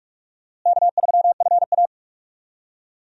Morse code, heard as a keyed tone, about 46 wpm, 700 Hz. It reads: K3FA
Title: USA Call Signs - 4 Characters - Repeated 3x - 45wpm